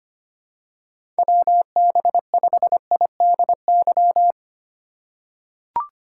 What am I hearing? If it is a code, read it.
Morse code, heard as a keyed tone, about 25 wpm, 700 Hz. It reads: WB5IDY
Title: K1USN SST QSO Practice - 25wpm